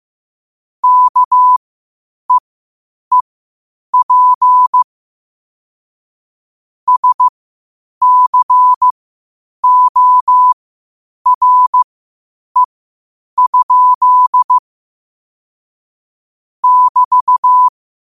Morse code, heard as a keyed tone, about 15 wpm, spaced as for 8 wpm, 1000 Hz. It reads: KEEP SCORE? <BT>
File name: qotdfiles / 8wpm